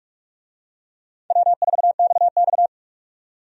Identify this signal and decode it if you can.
Morse code, heard as a keyed tone, about 46 wpm, 700 Hz. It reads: W4XX
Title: USA Call Signs - 4 Characters - 45wpm